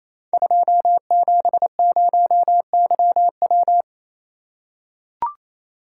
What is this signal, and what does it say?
Morse code, heard as a keyed tone, about 28 wpm, 700 Hz. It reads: 270YW
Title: Code Groups - 5 Letter:Numbers - 28wpm